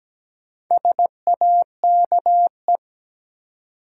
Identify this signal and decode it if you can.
Morse code, heard as a keyed tone, about 17 wpm, 700 Hz. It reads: SAKE
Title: Sets of 4 Words - New Word Drill - 17wpm